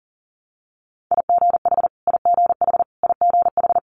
Morse code, heard as a keyed tone, about 38 wpm, 700 Hz. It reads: IGH IGH IGH
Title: Multi-letter Phonemes Repeated 3x 40wpm